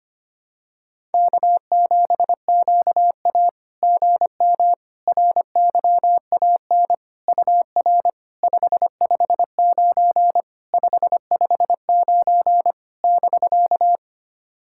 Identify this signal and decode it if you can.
Morse code, heard as a keyed tone, about 25 wpm, 660 Hz. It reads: K7QA GM RYAN UR 559 559 <BK>
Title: SOTA QSO Practice 25wpm